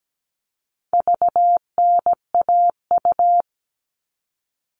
Morse code, heard as a keyed tone, about 17 wpm, 700 Hz. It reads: VNAU
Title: All Cut Numbers - 4-Digit 17wpm